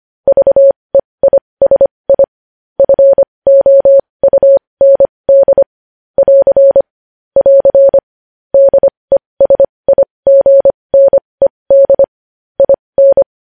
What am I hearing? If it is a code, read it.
Morse code, heard as a keyed tone, about 25 wpm, spaced as for 20 wpm, 550 Hz. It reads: VEISI FOUND <AR> <AR> DESIGNED IN